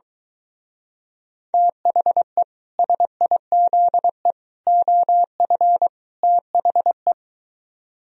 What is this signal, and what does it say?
Morse code, heard as a keyed tone, about 23 wpm, 700 Hz. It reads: THE SIZE OF THE